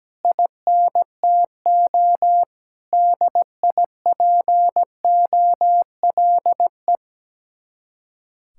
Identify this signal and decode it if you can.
Morse code, heard as a keyed tone, about 17 wpm, 700 Hz. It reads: INTO DIPOLE
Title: Speed-Racing Common QSO Phrases - 17wpm